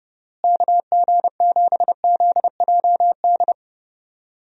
Morse code, heard as a keyed tone, about 31 wpm, 700 Hz. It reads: KG7ZJD
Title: USA Call Signs - 6 Characters - Repeated 3x - 30wpm